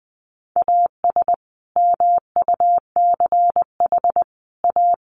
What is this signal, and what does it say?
Morse code, heard as a keyed tone, about 20 wpm, 700 Hz. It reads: AS MUCH A
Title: Sets of 5 Words Encore2 20wpm